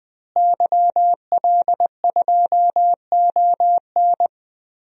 Morse code, heard as a keyed tone, about 20 wpm, 700 Hz. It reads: YL2ON
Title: DXCC-100 Call Signs - 5 Characters - Repeated 2x - 20wpm